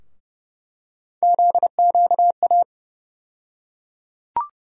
Morse code, heard as a keyed tone, about 30 wpm, 700 Hz. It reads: ZQA
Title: USA Call Sign Postfixes - 3 Letters 30wpm